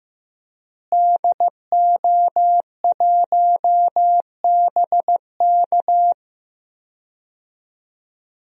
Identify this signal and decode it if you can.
Morse code, heard as a keyed tone, about 15 wpm, 700 Hz. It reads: DO1BK
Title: DXCC-100 Call Signs - 5 Characters - Repeated 3x - 15wpm